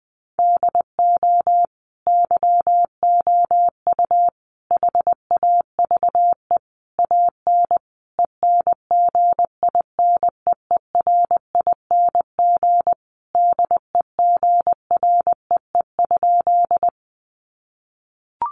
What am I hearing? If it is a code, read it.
Morse code, heard as a keyed tone, about 20 wpm, 700 Hz. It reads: DO YOU HAVE AN ENGINEERING DEGREE?
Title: Sentences from Top 1500 Words 20wpm